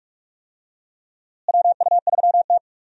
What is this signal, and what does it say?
Morse code, heard as a keyed tone, about 46 wpm, 700 Hz. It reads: WU3T
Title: USA Call Signs - 4 Characters - Repeated 2x - 45wpm